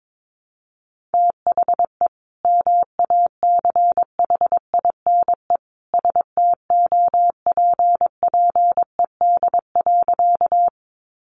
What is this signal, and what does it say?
Morse code, heard as a keyed tone, about 22 wpm, 700 Hz. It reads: THE MACHINE STOPPED.
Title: A Sound Of Thunder 22wpm